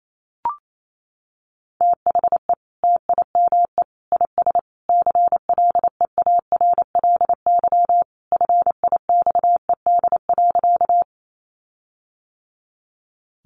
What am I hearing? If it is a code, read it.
Morse code, heard as a keyed tone, about 28 wpm, 700 Hz. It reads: THE TIME IS CLEARLY FIXED.
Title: Sentences from Top 1300 Words 28wpm